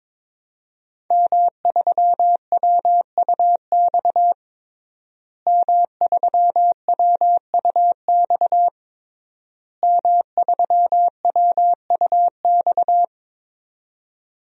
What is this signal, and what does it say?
Morse code, heard as a keyed tone, about 22 wpm, 700 Hz. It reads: M3WUX M3WUX M3WUX
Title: DXCC-100 Call Signs - 5 Characters - Repeated 3x - 22wpm